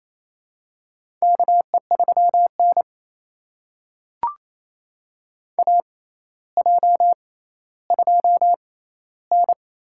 Morse code, heard as a keyed tone, about 28 wpm, spaced as for 14 wpm, 700 Hz. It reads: KE3N AJ2N
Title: USA Call Signs - 4 Characters - 6x Character Spacing - 28wpm